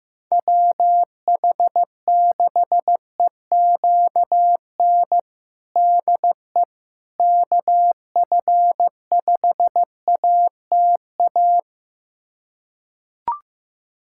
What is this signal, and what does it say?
Morse code, heard as a keyed tone, about 16 wpm, 700 Hz. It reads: WH6EQN DE KF5ATA
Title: USA Callsign DE Callsign - 6 Characters - 15wpm